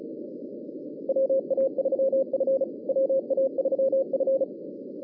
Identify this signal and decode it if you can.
Morse code, heard as a keyed tone, about 34 wpm, 560 Hz, with radio noise. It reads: WA3F WA3F